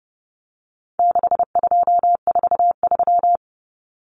Morse code, heard as a keyed tone, about 30 wpm, 700 Hz. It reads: 6243
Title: Numbers - 4-Digit 30wpm